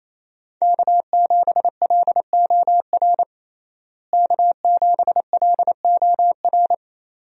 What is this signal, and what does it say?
Morse code, heard as a keyed tone, about 28 wpm, 700 Hz. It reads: K7LOR K7LOR
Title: USA Call Signs - 5 Characters - Repeated 2x - 28wpm